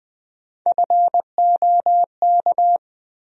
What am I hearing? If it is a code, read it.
Morse code, heard as a keyed tone, about 20 wpm, 700 Hz. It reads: FOK
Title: USA Call Sign Postfixes - 3 Letters 20wpm